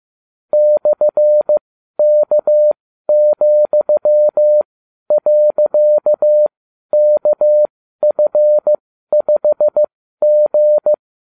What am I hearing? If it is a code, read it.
Morse code, heard as a keyed tone, about 15 wpm, spaced as for 12 wpm, 600 Hz. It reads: /K,.KF5G